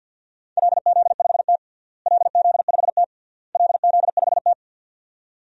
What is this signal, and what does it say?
Morse code, heard as a keyed tone, about 45 wpm, 700 Hz. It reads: LZ5T LZ5T LZ5T
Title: DXCC-100 Call Signs - 4 Characters - Repeated 3x - 50wpm